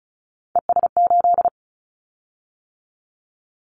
Morse code, heard as a keyed tone, about 36 wpm, 700 Hz. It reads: ES8
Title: Callsign Prefixes for Top 50 DXCC - 35wpm